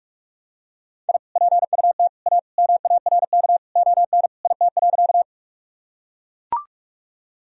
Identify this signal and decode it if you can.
Morse code, heard as a keyed tone, about 41 wpm, 700 Hz. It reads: I PUT A MARK ON IT.
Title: Sentences from Top 1000 Words - Review Part 1 - 45wpm